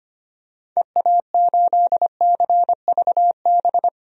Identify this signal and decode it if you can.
Morse code, heard as a keyed tone, about 25 wpm, 700 Hz. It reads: EA8CVB
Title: DXCC-100 Call Signs - 6 Characters - 25wpm